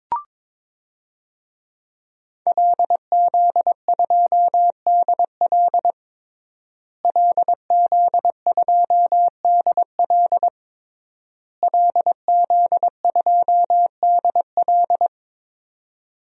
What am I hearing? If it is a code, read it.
Morse code, heard as a keyed tone, about 22 wpm, 700 Hz. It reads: LZ2DL LZ2DL LZ2DL
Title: DXCC-100 Call Signs - 5 Characters - Repeated 3x - 22wpm